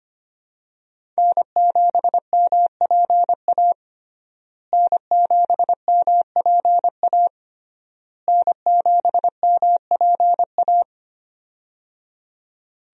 Morse code, heard as a keyed tone, about 25 wpm, 700 Hz. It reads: N7MPA N7MPA N7MPA
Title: USA Call Signs - 5 Characters - Repeated 3x - 25wpm